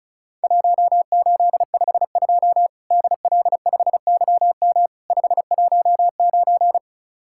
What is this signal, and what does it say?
Morse code, heard as a keyed tone, about 36 wpm, 700 Hz. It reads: 1852 DL5YM 519
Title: Worked All Europe DX Contest - QTC Exchange - 35wpm